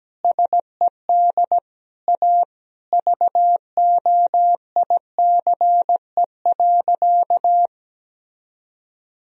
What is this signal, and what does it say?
Morse code, heard as a keyed tone, about 17 wpm, 700 Hz. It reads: SED A VOICE.
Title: Sentences from Top 500 Words and Abbreviations 17wpm